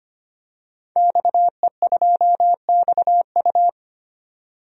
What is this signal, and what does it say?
Morse code, heard as a keyed tone, about 25 wpm, 700 Hz. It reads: XE2XU
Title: DXCC-100 Call Signs - 5 Characters - 25wpm